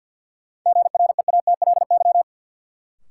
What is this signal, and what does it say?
Morse code, heard as a keyed tone, about 53 wpm, 700 Hz. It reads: GREATLY
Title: Sentences from Top 1400 Words - New Word Drill - 50wpm